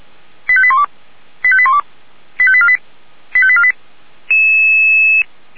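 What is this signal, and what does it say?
Five-tone selective call (ZVEI), heard as quick runs of five tones, tones of about 70 ms. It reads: 85812 85812 85848 85848
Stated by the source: ZVEI five-tone call 89155 received by SDR.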